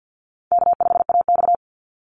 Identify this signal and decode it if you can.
Morse code, heard as a keyed tone, about 50 wpm, 700 Hz. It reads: K5AX
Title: Easy Call Signs - 50wpm